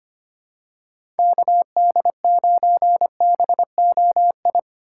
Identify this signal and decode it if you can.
Morse code, heard as a keyed tone, about 25 wpm, 690 Hz. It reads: KD9BOI
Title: USA Call Signs - 6 Characters - 3x Character Spacing - 25wpm